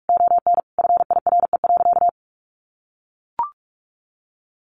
Morse code, heard as a keyed tone, about 44 wpm, 700 Hz. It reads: ON FIRE.
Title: Sentences from Top 1400 Words 45wpm